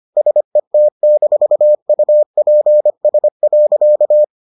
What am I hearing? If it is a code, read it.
Morse code, heard as a keyed tone, about 25 wpm, 600 Hz. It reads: SET-UPS.